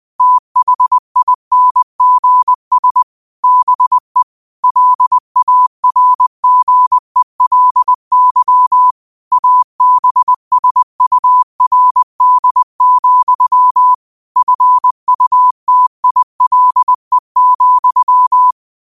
Morse code, heard as a keyed tone, about 20 wpm, 1000 Hz. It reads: THINGS BE LARGELY ABSURD, FUTILE,
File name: qotdfiles / 20wpm